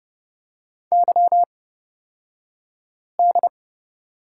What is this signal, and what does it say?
Morse code, heard as a keyed tone, about 30 wpm, spaced as for 5 wpm, 700 Hz. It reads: YD
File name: Lesson 38d - Letter q - Two Characters 30wpm